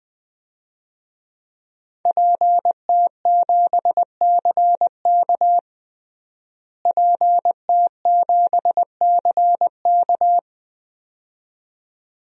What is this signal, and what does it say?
Morse code, heard as a keyed tone, about 21 wpm, 700 Hz. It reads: PT7CK PT7CK
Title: DXCC-100 Call Signs - 5 Characters - Repeated 2x - 20wpm